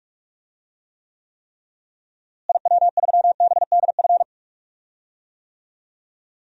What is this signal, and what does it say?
Morse code, heard as a keyed tone, about 46 wpm, 700 Hz. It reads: IW3BDF
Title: DXCC-100 Call Signs - 6 Characters - Repeated 3x - 45wpm